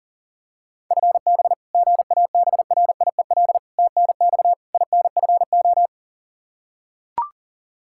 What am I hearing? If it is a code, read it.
Morse code, heard as a keyed tone, about 40 wpm, 700 Hz. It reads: FB GABRIEL TNX INFO
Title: Practice QSO - Exchange 3 - More Info - 40wpm